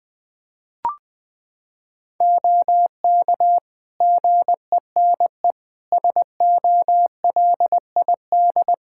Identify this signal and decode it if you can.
Morse code, heard as a keyed tone, about 20 wpm, 700 Hz. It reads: OK GENE SOLID
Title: Practice QSO - Exchange 3 - More Info - 20wpm